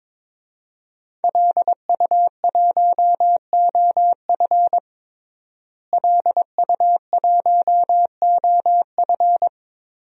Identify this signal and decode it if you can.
Morse code, heard as a keyed tone, about 22 wpm, 700 Hz. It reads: LU1OF LU1OF
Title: DXCC-100 Call Signs - 5 Characters - Repeated 3x - 22wpm